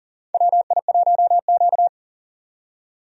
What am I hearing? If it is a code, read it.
Morse code, heard as a keyed tone, about 40 wpm, 700 Hz. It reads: WI1Q
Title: USA Call Signs - 4 Characters - 40wpm